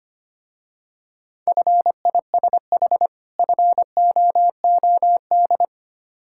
Morse code, heard as a keyed tone, about 25 wpm, 700 Hz. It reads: FISH FOOD
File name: practice / 25wpm / Difficult Words with Hints - 25wpm